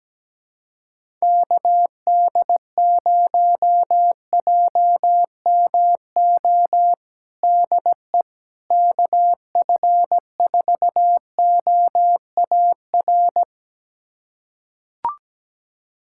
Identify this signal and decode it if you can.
Morse code, heard as a keyed tone, about 17 wpm, 700 Hz. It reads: KD0JMO DE KF4OAR
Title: USA Callsign DE Callsign - 6 Characters - 17wpm